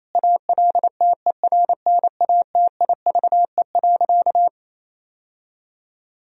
Morse code, heard as a keyed tone, about 28 wpm, 700 Hz. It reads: ALTERNATIVE.
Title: Sentences from Top 1000 Words - Review Part 4 - 28wpm